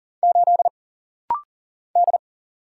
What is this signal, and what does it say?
Morse code, heard as a keyed tone, about 40 wpm, 700 Hz. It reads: 8 D
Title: Single Letter-Number Alt - Rapid-Fire 40wpm